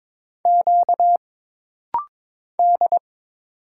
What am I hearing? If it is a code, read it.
Morse code, heard as a keyed tone, about 22 wpm, 700 Hz. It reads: Q D
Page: This rapid-fire timing is geared to push your ICR (Instant Character Recognition) skills.